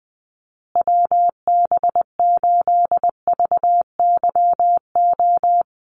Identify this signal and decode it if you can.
Morse code, heard as a keyed tone, about 20 wpm, 700 Hz. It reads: WB8VYO
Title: USA Call Signs Repeated 3x 20wpm